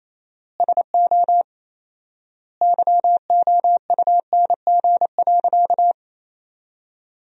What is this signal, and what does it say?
Morse code, heard as a keyed tone, about 28 wpm, 700 Hz. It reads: SO YOUNG.
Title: 4x Word-Spacing - Sentences from Top 400 Words - 28wpm